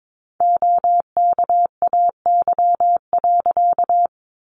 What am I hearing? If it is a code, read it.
Morse code, heard as a keyed tone, about 22 wpm, 700 Hz. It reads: OKAY.
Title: A Sound Of Thunder 22wpm